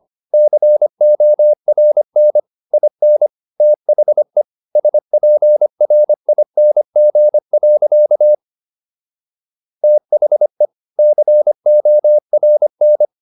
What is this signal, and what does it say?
Morse code, heard as a keyed tone, about 25 wpm, 600 Hz. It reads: CORN IN THE SPRING. THE CORN
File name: SS111 25